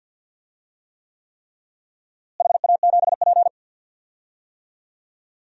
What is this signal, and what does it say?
Morse code, heard as a keyed tone, about 46 wpm, 700 Hz. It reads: HA7P
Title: DXCC-100 Call Signs - 4 Characters - Repeated 3x - 50wpm